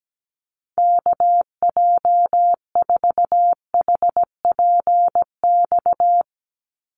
Morse code, heard as a keyed tone, about 17 wpm, 700 Hz. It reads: KJ4HPX
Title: USA Call Signs 17wpm